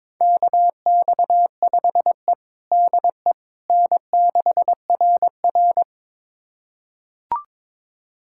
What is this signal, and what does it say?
Morse code, heard as a keyed tone, about 21 wpm, 700 Hz. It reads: KX5E DE N6RR
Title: USA Callsign DE Callsign - 4 Characters - 22wpm